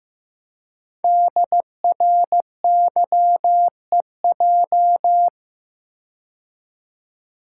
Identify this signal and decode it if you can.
Morse code, heard as a keyed tone, about 15 wpm, 700 Hz. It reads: DRYEJ
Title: Code Groups - 5 Letters - 15wpm